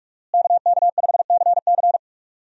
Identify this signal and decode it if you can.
Morse code, heard as a keyed tone, about 43 wpm, 700 Hz. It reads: KK5CC
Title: USA Call Signs - 5 Characters - Repeated 3x - 45wpm